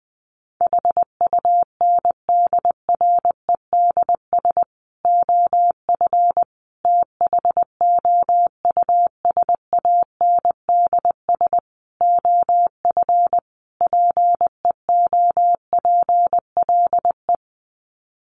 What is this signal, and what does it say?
Morse code, heard as a keyed tone, about 20 wpm, 700 Hz. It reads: HUNDREDS OF THOUSANDS OF PEOPLE
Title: Sets of 5 Words Encore2 20wpm